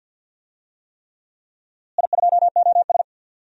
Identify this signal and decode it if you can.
Morse code, heard as a keyed tone, about 47 wpm, 700 Hz. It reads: I2OS
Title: DXCC-100 Call Signs - 4 Characters - Repeated 3x - 50wpm